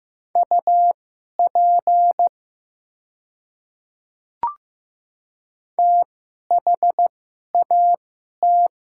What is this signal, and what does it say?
Morse code, heard as a keyed tone, about 15 wpm, spaced as for 11 wpm, 690 Hz. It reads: UP THAT
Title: 2x Character Spacing - Top 100 Words - 15wpm